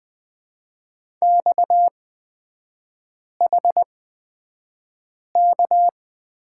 Morse code, heard as a keyed tone, about 20 wpm, spaced as for 5 wpm, 690 Hz. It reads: XHK
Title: 229 Lesson 37e - Letter x - Three Characters 20wpm